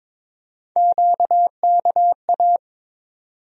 Morse code, heard as a keyed tone, about 22 wpm, 700 Hz. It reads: QKA
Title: USA Call Sign Postfixes - 3 Letters 22wpm